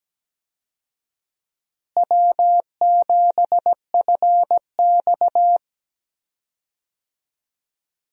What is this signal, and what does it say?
Morse code, heard as a keyed tone, about 17 wpm, 700 Hz. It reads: W7FX
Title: USA Call Signs - 4 Characters - Repeated 3x - 17wpm